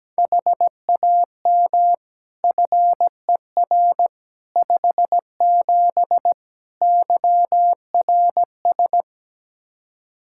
Morse code, heard as a keyed tone, about 17 wpm, 700 Hz. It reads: HAM FER 57 YRS
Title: K1USN SST QSO Practice 17wpm